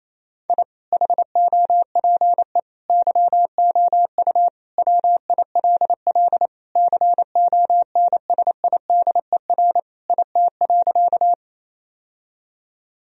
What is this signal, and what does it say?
Morse code, heard as a keyed tone, about 28 wpm, 700 Hz. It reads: I HOPE YOU WILL CONSIDER IT.
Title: Sentences from Top 1000 Words - Review Part 4 - 28wpm